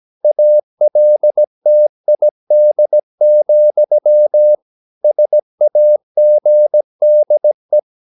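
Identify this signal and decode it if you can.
Morse code, heard as a keyed tone, about 17 wpm, 600 Hz. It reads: ALTID, SAGDE
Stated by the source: Slow (QRS) Morse practice at 17 wpm, from CW med Gnister.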